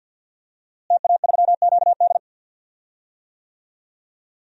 Morse code, heard as a keyed tone, about 50 wpm, 700 Hz. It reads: TA3QD
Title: DXCC-100 Call Signs - 5 Characters - Repeated 3x - 50wpm